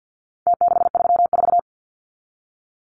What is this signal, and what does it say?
Morse code, heard as a keyed tone, about 49 wpm, 700 Hz. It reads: T634
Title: Numbers - Contest Style - 4-Digit 50wpm